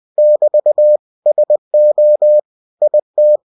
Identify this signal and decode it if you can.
Morse code, heard as a keyed tone, about 20 wpm, 600 Hz. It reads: <BT> SO IT
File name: cw-books / winnie-the-pooh / winnie-20wpm-page-0058